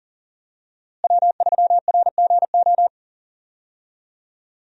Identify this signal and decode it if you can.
Morse code, heard as a keyed tone, about 37 wpm, 700 Hz. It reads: W3RGO
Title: USA Call Signs - 5 Characters - Repeated 3x - 40wpm